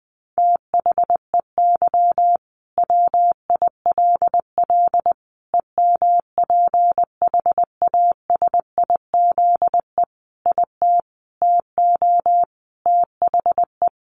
Morse code, heard as a keyed tone, about 20 wpm, 700 Hz. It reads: THEY WILL EMPHASIZE IT TO THE
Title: Sentences from Top 1500 Words 20wpm